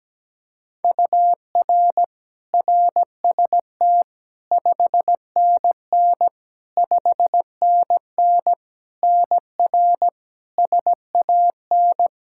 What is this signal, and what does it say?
Morse code, heard as a keyed tone, about 17 wpm, 700 Hz. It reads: UR RST 5NN 5NN NR SAN
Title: Practice QSO - Exchange 1 - Beginning - 17wpm